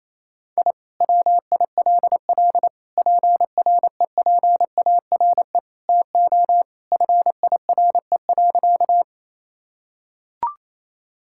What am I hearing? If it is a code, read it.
Morse code, heard as a keyed tone, about 28 wpm, 700 Hz. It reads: I WILL PREPARE TO FIRE.
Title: Sentences from Top 1000 Words - Review Part 6 - 28wpm